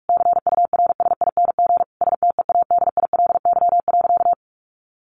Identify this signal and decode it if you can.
Morse code, heard as a keyed tone, about 44 wpm, 700 Hz. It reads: CURSING STEADILY.
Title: A Sound Of Thunder 45wpm